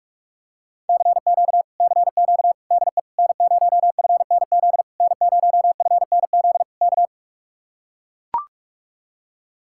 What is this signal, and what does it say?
Morse code, heard as a keyed tone, about 44 wpm, 700 Hz. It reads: CQ CQ DE N0FNZ N0FNZ K